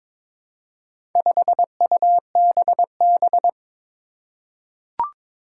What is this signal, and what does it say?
Morse code, heard as a keyed tone, about 23 wpm, 700 Hz. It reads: 5UBB
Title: Four Challenging Letter-Number Combinations 22wpm